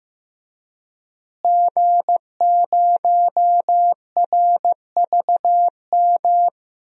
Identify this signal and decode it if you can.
Morse code, heard as a keyed tone, about 16 wpm, 700 Hz. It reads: G0RVM
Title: DXCC-100 Call Signs - 5 Characters - Repeated 3x - 15wpm